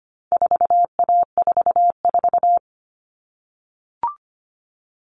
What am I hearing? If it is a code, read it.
Morse code, heard as a keyed tone, about 25 wpm, 700 Hz. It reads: 4A44
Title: All Cut Numbers - 4-Digit 25wpm